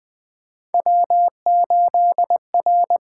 Morse code, heard as a keyed tone, about 21 wpm, 700 Hz. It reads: W8R
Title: USA Call Signs - 5 Characters - 8x Character Spacing - 20wpm